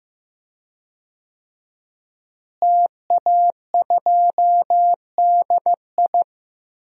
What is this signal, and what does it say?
Morse code, heard as a keyed tone, about 15 wpm, 700 Hz. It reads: TA2DI
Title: DXCC-100 Call Signs - 5 Characters - Repeated 2x - 15wpm